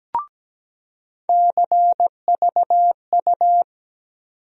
Morse code, heard as a keyed tone, about 17 wpm, 700 Hz. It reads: CVU
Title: USA Call Sign Postfixes - 3 Letters 17wpm